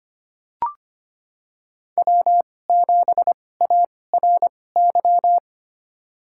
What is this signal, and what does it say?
Morse code, heard as a keyed tone, about 25 wpm, spaced as for 18 wpm, 700 Hz. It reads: W7ARY
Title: USA Call Signs - 5 Characters - 2x Character Spacing - 25wpm